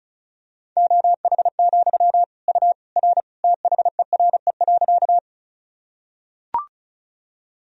Morse code, heard as a keyed tone, about 34 wpm, 700 Hz. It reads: OH, U R THERE.
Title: Sentences from Top 500 Words and Abbreviations 35wpm